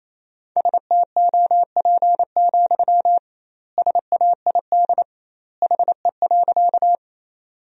Morse code, heard as a keyed tone, about 28 wpm, 700 Hz. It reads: STOP, SAID HE.
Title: 2x Word-Spacing - Sentences from Top 300 Words - 28wpm